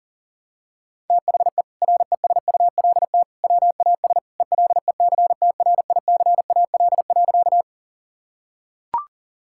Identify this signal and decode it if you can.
Morse code, heard as a keyed tone, about 40 wpm, 700 Hz. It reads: THE RESULT WAS ELECTRICAL.
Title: Sentences from Top 1000 Words - Review Part 6 - 40wpm